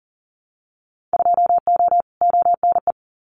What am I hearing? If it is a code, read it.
Morse code, heard as a keyed tone, about 40 wpm, 700 Hz. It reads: 2O ONE
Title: Winter Field Day Exchanges Encore 40wpm